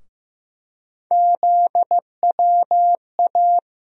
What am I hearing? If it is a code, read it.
Morse code, heard as a keyed tone, about 15 wpm, 700 Hz. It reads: ZWA
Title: USA Call Sign Postfixes - 3 Letters 15wpm